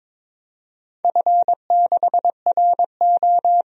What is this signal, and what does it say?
Morse code, heard as a keyed tone, about 23 wpm, 700 Hz. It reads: F6RO
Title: DXCC-100 Call Signs - 4 Characters - 22wpm